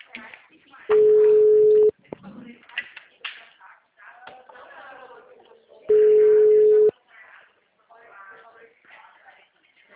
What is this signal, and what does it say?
Ringback tone (Europe, 425 Hz), with speech.
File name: out- -2010- - - .92137